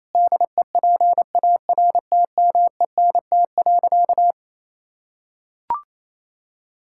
Morse code, heard as a keyed tone, about 28 wpm, 700 Hz. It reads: DEPARTMENT.
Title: Sentences from Top 1000 Words - Review Part 5 - 28wpm